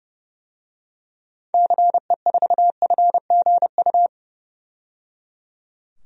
Morse code, heard as a keyed tone, about 30 wpm, 700 Hz. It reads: CE4FGU
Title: DXCC-100 Call Signs - 6 Characters - 30wpm